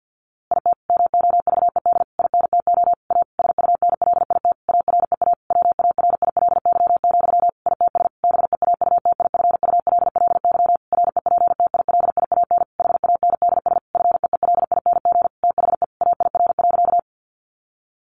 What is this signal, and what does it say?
Morse code, heard as a keyed tone, about 50 wpm, 700 Hz. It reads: IT MOVED INTO A SUNLIT AREA WARILY, ITS BEAUTIFULLY REPTILIAN HANDS FEELING THE AIR.
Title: A Sound Of Thunder 50wpm